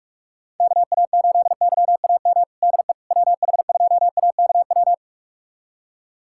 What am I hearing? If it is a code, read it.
Morse code, heard as a keyed tone, about 46 wpm, 700 Hz. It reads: KA8YAM DE WH2AKW
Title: USA Callsign DE Callsign - 6 Characters - 45wpm